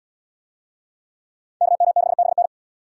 Morse code, heard as a keyed tone, about 67 wpm, 700 Hz. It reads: KM6BN